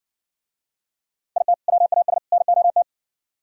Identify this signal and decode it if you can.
Morse code, heard as a keyed tone, about 60 wpm, 700 Hz. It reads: IT WAS NOT